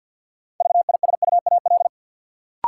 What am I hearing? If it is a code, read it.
Morse code, heard as a keyed tone, about 48 wpm, 700 Hz. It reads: VISUAL
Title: Sets of 5 Words Encore - New Word Drill - 50wpm